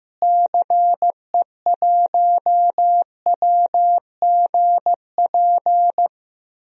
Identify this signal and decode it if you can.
Morse code, heard as a keyed tone, about 15 wpm, 700 Hz. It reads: CE1WGP
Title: DXCC-100 Call Signs - 6 Characters - Repeated 3x - 15wpm